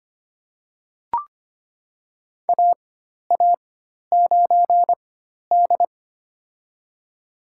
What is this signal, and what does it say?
Morse code, heard as a keyed tone, about 25 wpm, spaced as for 12 wpm, 700 Hz. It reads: AA9D
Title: USA Call Signs - 4 Characters - 4x Character Spacing - 25wpm